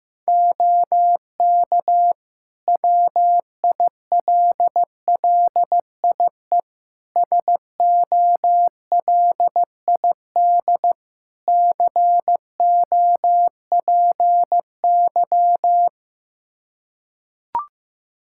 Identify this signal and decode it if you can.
Morse code, heard as a keyed tone, about 16 wpm, 700 Hz. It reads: OK WILLIE SOLID COPY
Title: Practice QSO - Exchange 3 - More Info - 15wpm